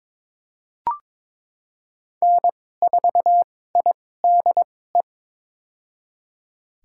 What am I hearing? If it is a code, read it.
Morse code, heard as a keyed tone, about 22 wpm, spaced as for 16 wpm, 700 Hz. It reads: N4IDE
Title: USA Call Signs - 5 Characters - 2x Character Spacing - 22wpm